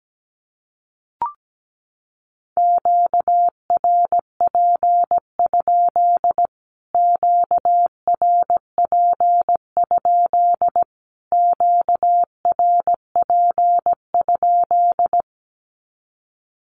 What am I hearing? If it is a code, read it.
Morse code, heard as a keyed tone, about 17 wpm, 700 Hz. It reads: QRP? QRP? QRP?
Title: Q-Codes Repeated 3x 17wpm